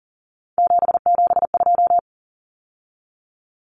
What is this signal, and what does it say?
Morse code, heard as a keyed tone, about 40 wpm, 700 Hz. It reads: 772
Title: Numbers - 3-Digit 40wpm